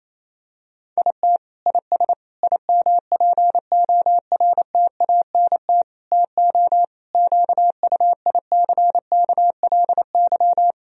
Morse code, heard as a keyed tone, about 28 wpm, 700 Hz. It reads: IT IS IMPORTANT TO QUICKLY
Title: Sentences from Top 1000 Words - Review Part 4 - 28wpm